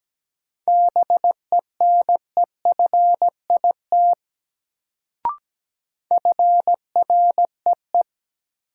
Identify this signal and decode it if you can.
Morse code, heard as a keyed tone, about 17 wpm, 700 Hz. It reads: BENEFIT FREE
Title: Sets of 5 Words Encore - New Word Drill - 17wpm